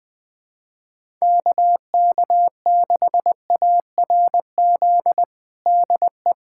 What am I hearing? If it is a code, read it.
Morse code, heard as a keyed tone, about 20 wpm, 700 Hz. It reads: KK6ARZ DE
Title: USA Callsign DE Callsign 20wpm